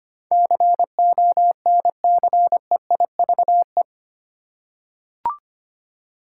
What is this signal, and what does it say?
Morse code, heard as a keyed tone, about 25 wpm, 700 Hz. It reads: CONCEIVE
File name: Eight Letter Words - 25wpm